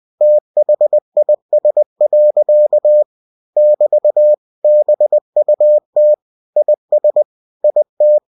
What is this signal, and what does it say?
Morse code, heard as a keyed tone, about 20 wpm, 600 Hz. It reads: THIS. <BT> BUT IS IT
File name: cw-books / winnie-the-pooh / winnie-20wpm-page-0047